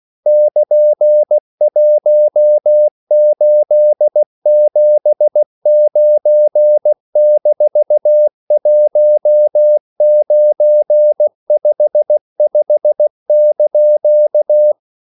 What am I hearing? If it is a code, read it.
Morse code, heard as a keyed tone, about 16 wpm, 600 Hz. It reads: <KN>1879-1955)
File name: CW / Practice / EnglishQuotations0116